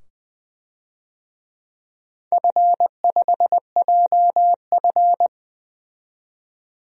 Morse code, heard as a keyed tone, about 21 wpm, 700 Hz. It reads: F5JF
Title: DXCC-100 Call Signs - 4 Characters - 20wpm